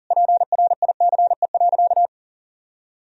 Morse code, heard as a keyed tone, about 40 wpm, 700 Hz. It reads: PRICE.
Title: Sentences from Top 1000 Words - Review Part 4 - 40wpm